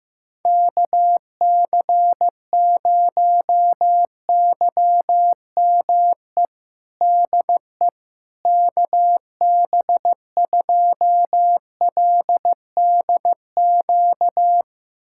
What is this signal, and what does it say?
Morse code, heard as a keyed tone, about 15 wpm, 700 Hz. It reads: KC0YME DE KB2LDQ
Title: USA Callsign DE Callsign - 6 Characters - 15wpm